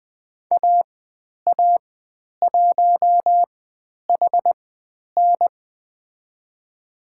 Morse code, heard as a keyed tone, about 20 wpm, spaced as for 10 wpm, 700 Hz. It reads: AA1HN
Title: 095 Lesson 16g - Letter c - Call Signs 20wpm